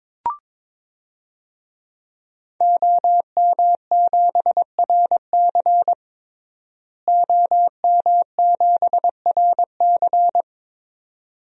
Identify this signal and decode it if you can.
Morse code, heard as a keyed tone, about 22 wpm, 700 Hz. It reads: OM7RC OM7RC
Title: DXCC-100 Call Signs - 5 Characters - Repeated 2x - 22wpm